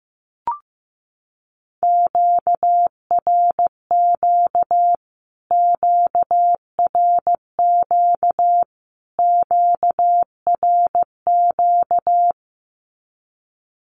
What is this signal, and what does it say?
Morse code, heard as a keyed tone, about 15 wpm, 700 Hz. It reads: QRQ QRQ QRQ
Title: Q-Codes Repeated 3x 15wpm